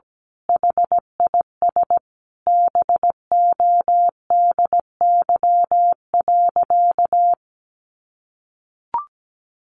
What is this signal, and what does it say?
Morse code, heard as a keyed tone, about 17 wpm, 700 Hz. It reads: HIS BODY.
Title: A Sound Of Thunder 17wpm